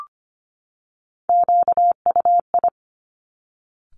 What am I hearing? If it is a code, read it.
Morse code, heard as a keyed tone, about 25 wpm, 700 Hz. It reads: QUI
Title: Top Three Letter Combinations in Words 25wpm